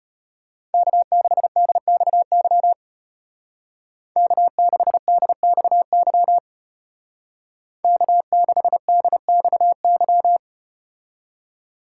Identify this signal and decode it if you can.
Morse code, heard as a keyed tone, about 32 wpm, 700 Hz. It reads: K6DXY K6DXY K6DXY